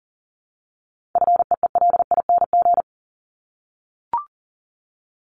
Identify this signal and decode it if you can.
Morse code, heard as a keyed tone, about 40 wpm, 700 Hz. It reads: FEELING
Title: Top 1001-1250 Words 40wpm